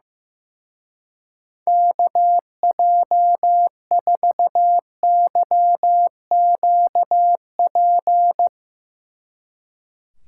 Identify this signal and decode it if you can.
Morse code, heard as a keyed tone, about 16 wpm, 700 Hz. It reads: KJ4YQP
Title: Speed Racing - USA 2x3 Call Signs 15wpm